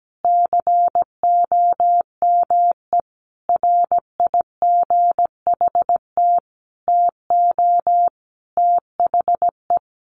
Morse code, heard as a keyed tone, about 17 wpm, 700 Hz. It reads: COME RIGHT TO THE
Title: Sentences from Top 1400 Words 17wpm